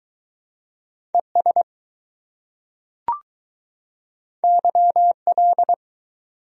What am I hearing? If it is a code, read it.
Morse code, heard as a keyed tone, about 20 wpm, 700 Hz. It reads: ES YL